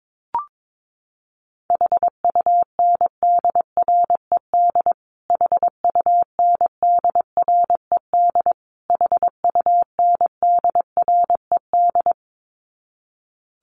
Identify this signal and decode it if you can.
Morse code, heard as a keyed tone, about 22 wpm, 700 Hz. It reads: HUNDRED HUNDRED HUNDRED
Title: Top 500 Words Repeated 22wpm